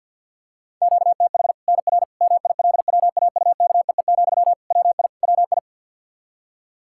Morse code, heard as a keyed tone, about 51 wpm, 700 Hz. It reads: QTH NR MILWAUKEE, WI WI
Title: K1USN SST QSO Practice 50wpm